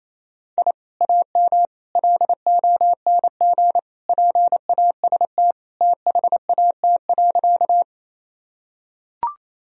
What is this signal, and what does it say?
Morse code, heard as a keyed tone, about 28 wpm, 700 Hz. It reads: I AM LONG PAST THAT.
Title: Sentences from Top 1000 Words - Review Part 3 - 28wpm